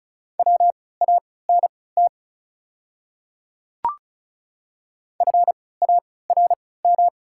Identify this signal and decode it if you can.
Morse code, heard as a keyed tone, about 36 wpm, spaced as for 20 wpm, 690 Hz. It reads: WANT FARM
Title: 3x Character Spacing - Top 500 Words - 35wpm